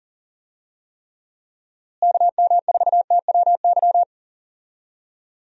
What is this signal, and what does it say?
Morse code, heard as a keyed tone, about 40 wpm, 700 Hz. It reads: KM4TWY
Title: USA Call Signs - 6 Characters - Repeated 2x - 40wpm